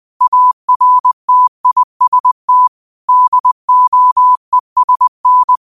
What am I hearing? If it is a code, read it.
Morse code, heard as a keyed tone, about 20 wpm, 1000 Hz. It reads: ARTIST DOESN